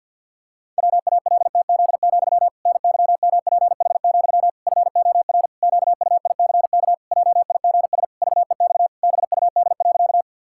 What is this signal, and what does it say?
Morse code, heard as a keyed tone, about 52 wpm, 700 Hz. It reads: WALTZ, NYMPH, FOR QUICK JIGS VEX BUD.
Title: Pangram Challenge v2 - 50wpm